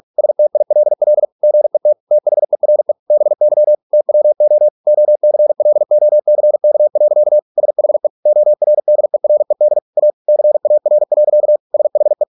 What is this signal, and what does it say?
Morse code, heard as a keyed tone, about 46 wpm, 600 Hz. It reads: STILL GET THERE BY TWO OCLOCK. SHE ORDERED A CAR. SHE